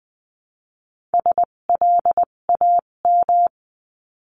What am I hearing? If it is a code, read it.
Morse code, heard as a keyed tone, about 20 wpm, spaced as for 17 wpm, 700 Hz. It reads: SLAM
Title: 257 Lesson 42b - Words 14-20wpm